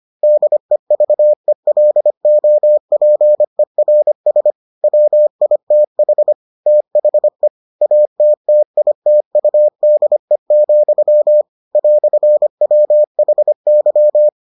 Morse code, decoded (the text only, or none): DEVELOPERS WITH THE ATTITUDE, "WHY